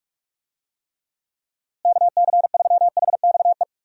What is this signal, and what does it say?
Morse code, heard as a keyed tone, about 41 wpm, 700 Hz. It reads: KC3HXE
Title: USA Call Signs - 6 Characters - Repeated 3x - 45wpm